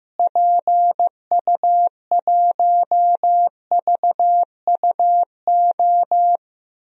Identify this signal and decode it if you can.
Morse code, heard as a keyed tone, about 15 wpm, 700 Hz. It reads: PU1VUO
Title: DXCC-100 Call Signs - 6 Characters - Repeated 3x - 15wpm